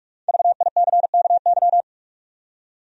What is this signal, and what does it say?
Morse code, heard as a keyed tone, about 45 wpm, 700 Hz. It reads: VICKY
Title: USA Common Names - Max 5 Letters - 45wpm